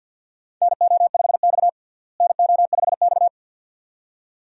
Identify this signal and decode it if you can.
Morse code, heard as a keyed tone, about 49 wpm, 700 Hz. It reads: NO5X NO5X
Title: USA Call Signs - 4 Characters - Repeated 2x - 50wpm